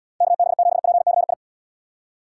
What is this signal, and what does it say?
Morse code, heard as a keyed tone, about 70 wpm, 700 Hz. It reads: KB7PCI